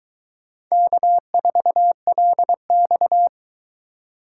Morse code, heard as a keyed tone, about 22 wpm, 700 Hz. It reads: K4LX